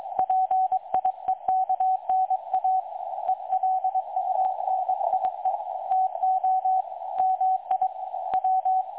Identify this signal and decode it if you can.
Morse code, heard as a keyed tone, about 22 wpm, 750 Hz, with radio noise. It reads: PIEKNA ELUNIE TO MI W
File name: dlazaawansowanych / nagraniazpasma / websdr recording 3565.0kHz